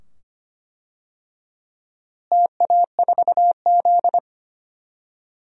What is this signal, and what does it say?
Morse code, heard as a keyed tone, about 25 wpm, 700 Hz. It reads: TA4Z
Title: DXCC-100 Call Signs - 4 Characters - Repeated 2x - 25wpm